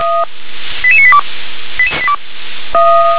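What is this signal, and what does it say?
Five-tone selective call (ZVEI), heard as quick runs of five tones, tones of about 70 ms. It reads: 88982
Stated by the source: ZVEI five-tone selective call received on SDR.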